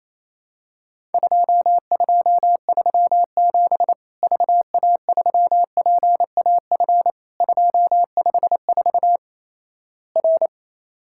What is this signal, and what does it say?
Morse code, heard as a keyed tone, about 28 wpm, 700 Hz. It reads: 2237 VA3PAF 254 R
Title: Worked All Europe DX Contest - QTC Exchange - 28wpm